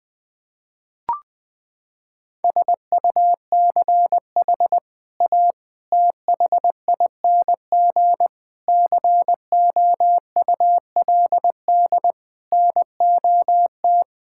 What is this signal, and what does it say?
Morse code, heard as a keyed tone, about 21 wpm, 700 Hz. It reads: SUCH A THING COULD NOT
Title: Sentences from Top 1000 Words - Review Part 6 - 20wpm